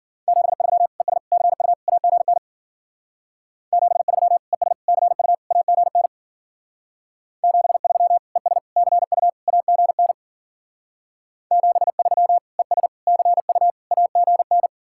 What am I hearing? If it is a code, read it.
Morse code, heard as a keyed tone, about 50 wpm, 700 Hz. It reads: 73 ES CU AGN 73 ES CU AGN 73 ES CU AGN 73 ES CU AGN
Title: Speed-Racing Common QSO Phrases - 40wpm